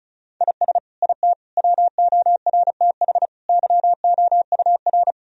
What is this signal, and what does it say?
Morse code, heard as a keyed tone, about 36 wpm, 700 Hz. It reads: IS IT WORTH YOUR
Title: Sentences from Top 1000 Words - Review Part 1 - 35wpm